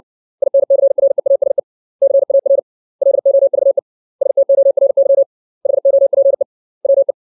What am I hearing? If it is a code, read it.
Morse code, heard as a keyed tone, about 60 wpm, 550 Hz. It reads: INCREASE CAR LOVE STORY HOPE WE
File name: 632 words / 632 Words 60WPM